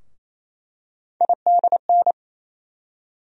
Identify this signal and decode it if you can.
Morse code, heard as a keyed tone, about 28 wpm, 700 Hz. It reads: IDN
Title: USA Call Sign Postfixes - 3 Letters 28wpm